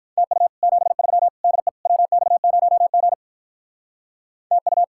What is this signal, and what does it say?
Morse code, heard as a keyed tone, about 46 wpm, 700 Hz. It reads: TU 73 DE WX0G TU